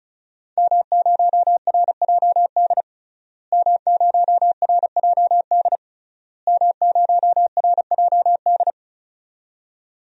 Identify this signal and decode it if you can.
Morse code, heard as a keyed tone, about 34 wpm, 700 Hz. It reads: M0RJD M0RJD M0RJD
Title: DXCC-100 Call Signs - 5 Characters - Repeated 3x - 35wpm